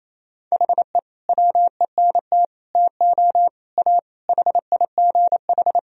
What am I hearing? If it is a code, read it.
Morse code, heard as a keyed tone, about 28 wpm, 700 Hz. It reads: HE WENT TO A HIGH
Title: Sentences from Top 1000 Words - Review Part 1 - 28wpm